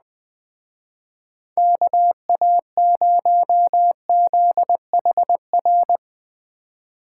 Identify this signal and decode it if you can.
Morse code, heard as a keyed tone, about 20 wpm, 700 Hz. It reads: KA0ZHR